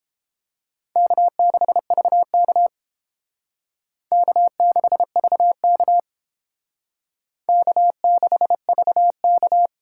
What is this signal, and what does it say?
Morse code, heard as a keyed tone, about 30 wpm, 700 Hz. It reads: K6VK K6VK K6VK